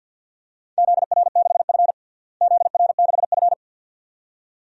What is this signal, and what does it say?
Morse code, heard as a keyed tone, about 51 wpm, 700 Hz. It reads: ZR6F ZR6F
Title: DXCC-100 Call Signs - 4 Characters - Repeated 2x - 50wpm